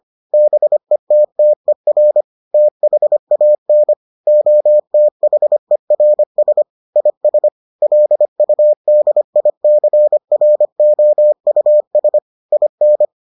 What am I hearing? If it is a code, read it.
Morse code, heard as a keyed tone, about 25 wpm, 600 Hz. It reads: BETTER THAN OTHERS IS LUDICROUS IN